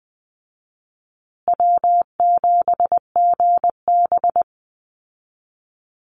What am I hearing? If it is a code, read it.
Morse code, heard as a keyed tone, about 20 wpm, 700 Hz. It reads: W7GB
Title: Easy Call Signs Repeated 2x - 20wpm